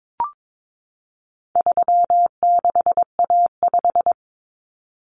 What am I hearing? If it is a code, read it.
Morse code, heard as a keyed tone, about 22 wpm, 700 Hz. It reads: 36A5
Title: Common Cut Numbers - 4-Digit 22wpm